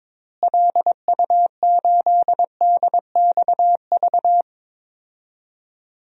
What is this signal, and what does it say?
Morse code, heard as a keyed tone, about 22 wpm, 700 Hz. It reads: LU8DXV
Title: DXCC-100 Call Signs - 6 Characters - Repeated 2x - 22wpm